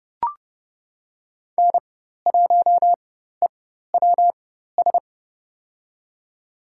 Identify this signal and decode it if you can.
Morse code, heard as a keyed tone, about 30 wpm, spaced as for 14 wpm, 700 Hz. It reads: N1EWS
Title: USA Call Signs - 5 Characters - 4x Character Spacing - 30wpm